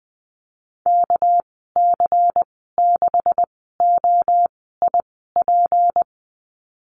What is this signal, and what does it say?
Morse code, heard as a keyed tone, about 20 wpm, spaced as for 14 wpm, 700 Hz. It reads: KC6OIP
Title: 213 Lesson 34g - Number 8 - Call Signs 20wpm